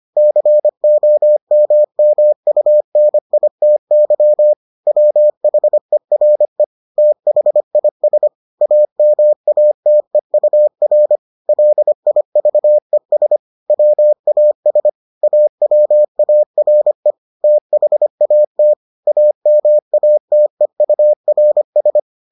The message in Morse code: COMMUNITY WHERE THIS AMATEUR LIVES WAS AWARE THAT AMATEURS

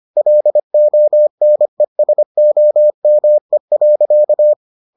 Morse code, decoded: LONESOME.

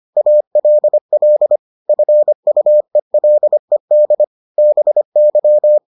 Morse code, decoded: ALL FUELED BY

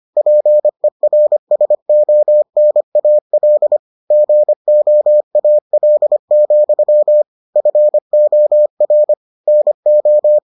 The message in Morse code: PERSONAL GOAL, FOR NO

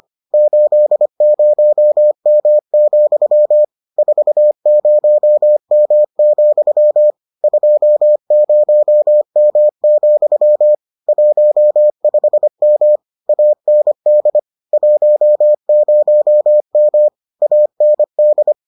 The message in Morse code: 80M, 40M, 20M, 15M AND 10M AND

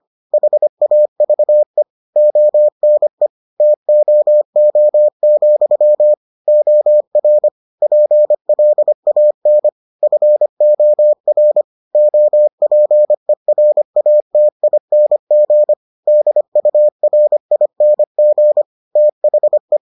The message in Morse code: HAVE ONE TOO, OR PLAN FOR OPERATING DURING THE

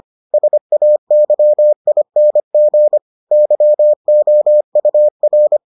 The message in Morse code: SAYING YOUR

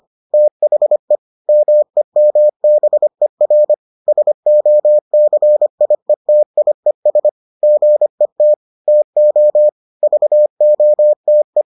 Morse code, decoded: THE MEMBER SOCIETIES GET TO VOTE

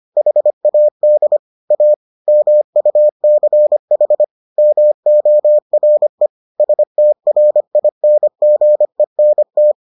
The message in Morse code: HAD A MUCH MORE STRINGENT

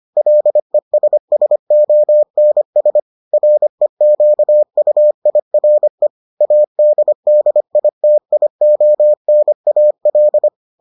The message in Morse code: LESSONS REQUIRE ADDITIONAL

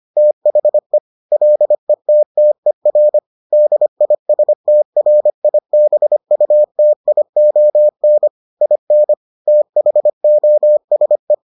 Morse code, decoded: THE LETTER DISTRIBUTION IN THOSE

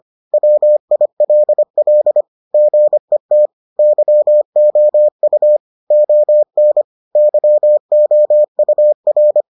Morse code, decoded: WILL GET YOU ON YOUR

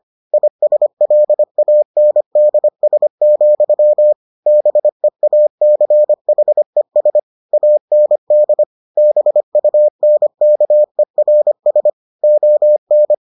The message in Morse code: ISLANDS, BEACHES AND BUNKERS ON

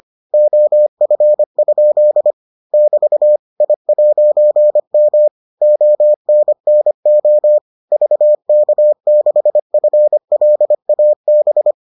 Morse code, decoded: OF? <BT> I'M ONNO VK6FLAB

